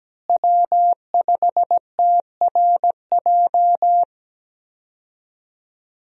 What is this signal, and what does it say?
Morse code, heard as a keyed tone, about 17 wpm, 700 Hz. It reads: W5TRJ
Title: USA Call Signs - 5 Characters - Repeated 3x - 17wpm